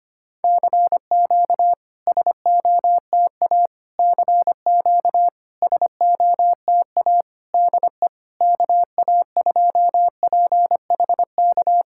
Morse code, heard as a keyed tone, about 25 wpm, 700 Hz. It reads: CQ SOTA CQ SOTA DE KA2PHK
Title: SOTA QSO Practice 25wpm